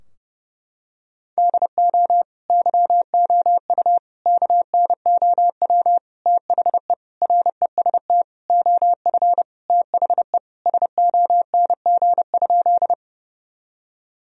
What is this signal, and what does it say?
Morse code, heard as a keyed tone, about 31 wpm, 700 Hz. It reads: DO YOU KNOW THE REST OF THE SONG?
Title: Sentences from Top 1000 Words - Review Part 2 - 30wpm